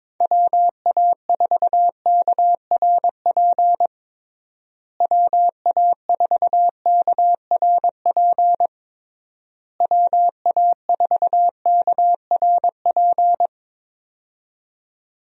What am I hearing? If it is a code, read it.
Morse code, heard as a keyed tone, about 21 wpm, 700 Hz. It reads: WA4KRP WA4KRP WA4KRP
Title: USA Call Signs - 6 Characters - Repeated 3x - 22wpm